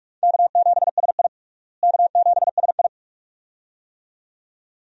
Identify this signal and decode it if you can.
Morse code, heard as a keyed tone, about 43 wpm, 700 Hz. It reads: K7SI K7SI
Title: USA Call Signs - 4 Characters - Repeated 2x - 45wpm